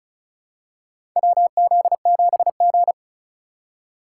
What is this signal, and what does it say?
Morse code, heard as a keyed tone, about 34 wpm, 700 Hz. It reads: WZ7G
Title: USA Call Signs - 4x Character Spacing - 35wpm